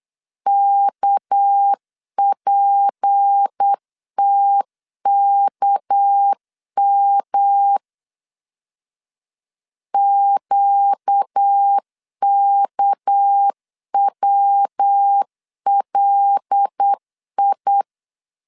Morse code, heard as a keyed tone, about 8 wpm, 800 Hz. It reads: KPTKM QKWLI
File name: LEZIONI / 6 PKX0 / 6 PK 8wpm 3003